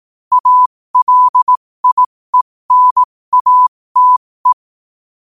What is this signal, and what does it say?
Morse code, heard as a keyed tone, about 18 wpm, spaced as for 15 wpm, 1000 Hz. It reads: ALIENATE
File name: qotdfiles / 15wpm